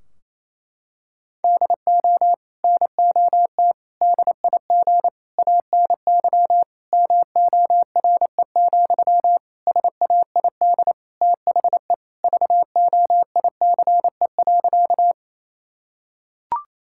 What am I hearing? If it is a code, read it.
Morse code, heard as a keyed tone, about 28 wpm, 700 Hz. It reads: DO NOT DIG ANY MORE, SAID THE VOICE.
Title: Sentences from Top 1600 Words 28wpm